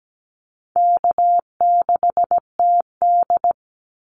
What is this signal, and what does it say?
Morse code, heard as a keyed tone, about 17 wpm, 700 Hz. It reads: K6TD
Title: Easy Call Signs Repeated 2x - 17wpm